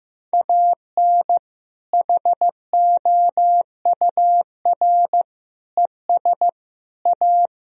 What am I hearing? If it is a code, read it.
Morse code, heard as a keyed tone, about 15 wpm, 700 Hz. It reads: AN HOUR ES A
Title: Sentences from Top 400 Words and Abbreviations 15wpm